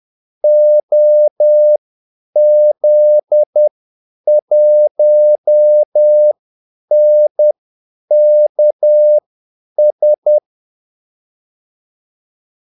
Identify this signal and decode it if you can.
Morse code, heard as a keyed tone, about 10 wpm, spaced as for 8 wpm, 600 Hz. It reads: OZ1NKS
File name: Callsigns-10-8wpm